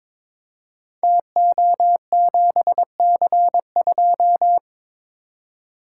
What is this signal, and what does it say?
Morse code, heard as a keyed tone, about 22 wpm, 700 Hz. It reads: TO7C2
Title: Code Groups - 5 Letter:Numbers - 22wpm